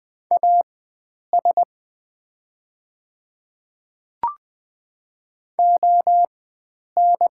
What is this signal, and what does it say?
Morse code, heard as a keyed tone, about 21 wpm, spaced as for 9 wpm, 700 Hz. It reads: AS ON
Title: 4x Character Spacing - Top 100 Words - 20wpm